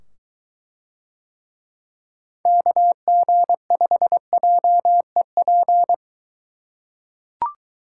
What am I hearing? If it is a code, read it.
Morse code, heard as a keyed tone, about 22 wpm, 700 Hz. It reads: KG5JEP